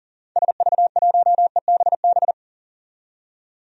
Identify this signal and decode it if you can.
Morse code, heard as a keyed tone, about 40 wpm, 700 Hz. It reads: SV1EBB
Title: DXCC-100 Call Signs - 6 Characters - 40wpm